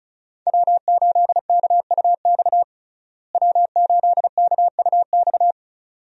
Morse code, heard as a keyed tone, about 36 wpm, 700 Hz. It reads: W8KUX W8KUX
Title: USA Call Signs - 5 Characters - Repeated 2x - 35wpm